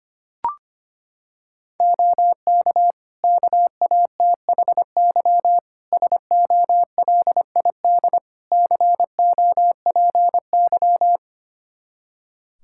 Morse code, heard as a keyed tone, about 25 wpm, 700 Hz. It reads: OK KATHY SOLID COPY
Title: Practice QSO - Exchange 3 - More Info - 25wpm